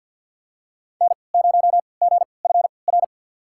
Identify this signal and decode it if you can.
Morse code, heard as a keyed tone, about 53 wpm, spaced as for 29 wpm, 700 Hz. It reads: N0GFR
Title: USA Call Signs - 5 Characters - 3x Character Spacing - 50wpm